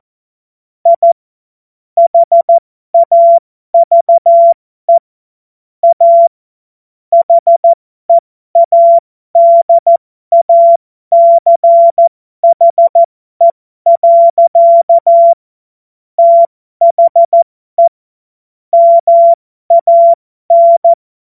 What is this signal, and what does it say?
Morse code, heard as a keyed tone, about 14 wpm, spaced as for 12 wpm, 680 Hz. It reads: I HAVE A HEADACHE. THE MAN
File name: Morsekurs / Witze